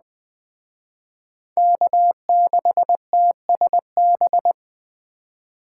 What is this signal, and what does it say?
Morse code, heard as a keyed tone, about 23 wpm, 700 Hz. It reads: K6TSB
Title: Speed Racing - USA 1x3 Call Signs 20wpm